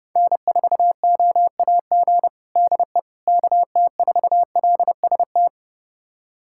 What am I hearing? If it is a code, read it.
Morse code, heard as a keyed tone, about 32 wpm, 700 Hz. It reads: N4OAG DE KT4LST
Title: USA Callsign DE Callsign 30wpm